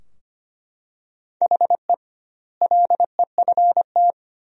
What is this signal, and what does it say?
Morse code, heard as a keyed tone, about 25 wpm, 700 Hz. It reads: HE LEFT
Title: 2x Word-Spacing - Sentences from Top 500 Words - 25wpm